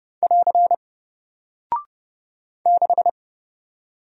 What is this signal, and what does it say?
Morse code, heard as a keyed tone, about 30 wpm, 700 Hz. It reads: <AR> 6
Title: Single Character (Letter:Number:Punctuation:Prosign) Alt 30wpm